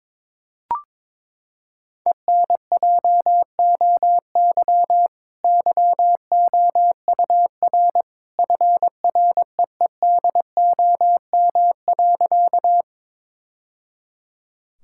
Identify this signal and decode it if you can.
Morse code, heard as a keyed tone, about 22 wpm, 700 Hz. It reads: ENJOY YOUR FREEDOM.
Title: Sentences from Top 1000 Words - Review Part 5 - 22wpm